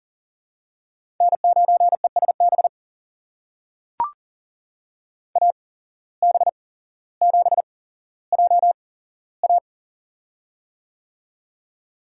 Morse code, heard as a keyed tone, about 40 wpm, spaced as for 17 wpm, 700 Hz. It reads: N9ESB AB7JA
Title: USA Call Signs - 5 Characters - 8x Character Spacing - 40wpm